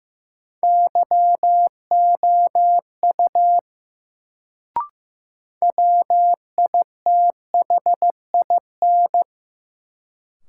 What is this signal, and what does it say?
Morse code, heard as a keyed tone, about 15 wpm, 700 Hz. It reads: YOU WITHIN
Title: Sets of 2 Words - New Word Drill - 15wpm